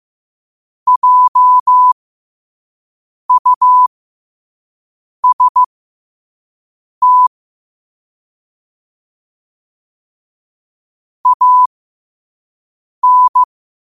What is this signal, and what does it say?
Morse code, heard as a keyed tone, about 15 wpm, spaced as for 5 wpm, 1000 Hz. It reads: JUST AN
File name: qotdfiles / 5wpm